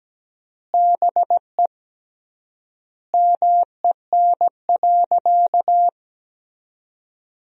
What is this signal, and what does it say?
Morse code, heard as a keyed tone, about 17 wpm, 700 Hz. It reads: BE MEN.
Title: 3x Word-Spacing - Sentences from Top 300 Words - 17wpm